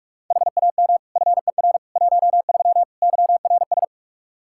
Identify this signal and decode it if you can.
Morse code, heard as a keyed tone, about 44 wpm, 700 Hz. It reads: HAM FER 13 YRS
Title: Practice QSO - Exchange 3 - More Info - 45wpm